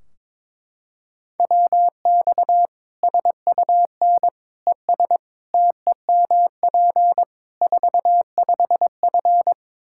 Morse code, heard as a keyed tone, about 23 wpm, 700 Hz. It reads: WX SUN ES TEMP 45F
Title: Practice QSO - Exchange 2 - Info - 22wpm